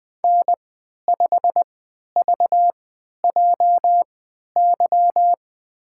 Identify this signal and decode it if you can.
Morse code, heard as a keyed tone, about 20 wpm, spaced as for 11 wpm, 700 Hz. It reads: N5VJY
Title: USA Call Signs - 3x Character Spacing - 20wpm